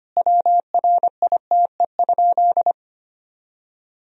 Morse code, heard as a keyed tone, about 25 wpm, 700 Hz. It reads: WRITE?
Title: Sentences from Top 1000 Words - Review Part 1 - 25wpm